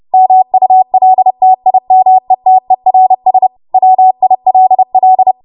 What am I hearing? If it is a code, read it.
Morse code, heard as a keyed tone, about 30 wpm, 750 Hz. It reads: MULTIMETERS WILL
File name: Archive / 30 WPM / 30WPM